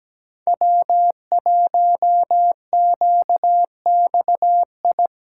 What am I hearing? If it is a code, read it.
Morse code, heard as a keyed tone, about 17 wpm, 700 Hz. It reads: W1QXI
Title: USA Callsign DE Callsign - 5 Characters - 17wpm